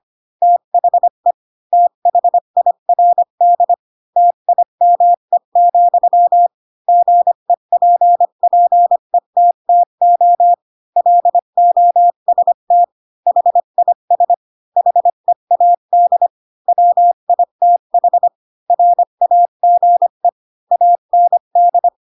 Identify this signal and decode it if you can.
Morse code, heard as a keyed tone, about 25 wpm, 700 Hz. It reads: THE THIRD TIME, GEPPETTO LOST HIS HEAD WITH RAGE AND